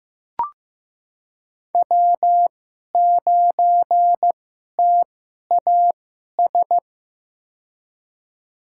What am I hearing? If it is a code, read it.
Morse code, heard as a keyed tone, about 15 wpm, spaced as for 11 wpm, 700 Hz. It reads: W9TAS
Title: USA Call Signs - 5 Characters - 2x Character Spacing - 15wpm